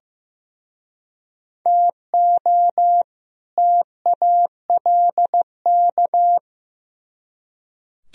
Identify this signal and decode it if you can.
Morse code, heard as a keyed tone, about 18 wpm, 700 Hz. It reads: TO TALK
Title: Speed-Racing - Sets of 2 Words 15wpm